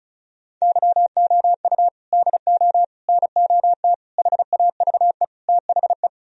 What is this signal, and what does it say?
Morse code, heard as a keyed tone, about 46 wpm, 700 Hz. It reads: YOU DO NOT HAVE THE